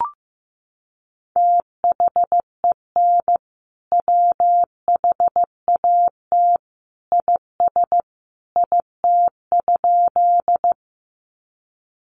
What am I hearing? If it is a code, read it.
Morse code, heard as a keyed tone, about 15 wpm, 700 Hz. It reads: THEN WHAT IS IT?
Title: Sentences from Top 100 Words 15wpm